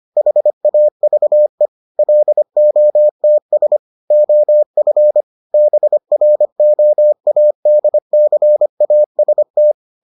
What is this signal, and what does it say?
Morse code, heard as a keyed tone, about 25 wpm, 600 Hz. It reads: HAVE LOTS OF BROADCAST